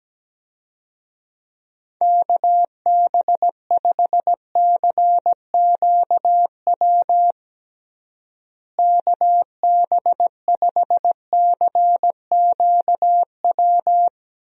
Morse code, heard as a keyed tone, about 17 wpm, 700 Hz. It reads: KB5CQW KB5CQW
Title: USA Call Signs - 6 Characters - Repeated 2x - 17wpm